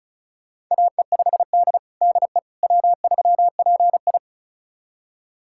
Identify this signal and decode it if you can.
Morse code, heard as a keyed tone, about 34 wpm, 700 Hz. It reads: AE5D DE W3PI
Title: USA Callsign DE Callsign - 4 Characters - 35wpm